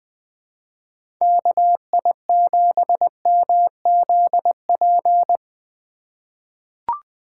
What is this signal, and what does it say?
Morse code, heard as a keyed tone, about 20 wpm, 700 Hz. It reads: KI7MZP
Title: USA Call Signs - 6 Characters - 6x Character Spacing - 20wpm